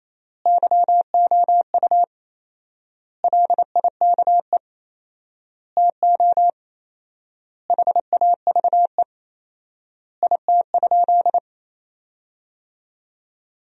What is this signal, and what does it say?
Morse code, heard as a keyed tone, about 27 wpm, 700 Hz. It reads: YOU LIKE TO HAVE IT?
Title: 4x Word-Spacing - Sentences from Top 100 Words - 28wpm